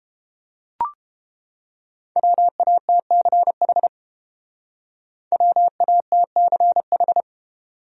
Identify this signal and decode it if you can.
Morse code, heard as a keyed tone, about 30 wpm, 700 Hz. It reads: WATCH WATCH